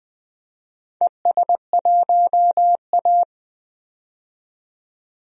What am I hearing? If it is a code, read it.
Morse code, heard as a keyed tone, about 20 wpm, 700 Hz. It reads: ES1A
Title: DXCC-100 Call Signs - 4 Characters - 20wpm